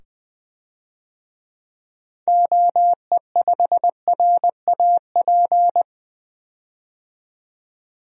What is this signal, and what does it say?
Morse code, heard as a keyed tone, about 20 wpm, 700 Hz. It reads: OE5RAP
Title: DXCC-100 Call Signs - 6 Characters - 20wpm